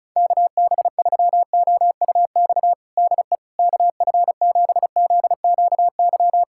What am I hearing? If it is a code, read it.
Morse code, heard as a keyed tone, about 34 wpm, 700 Hz. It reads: KB3OUX DE KF7ZQY
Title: USA Callsign DE Callsign - 6 Characters - 35wpm